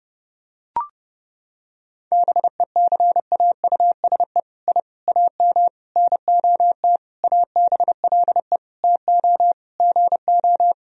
Morse code, heard as a keyed tone, about 32 wpm, 700 Hz. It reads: BECAUSE I AM NOT ABLE TO GO
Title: Sentences from Top 1000 Words - Review Part 3 - 30wpm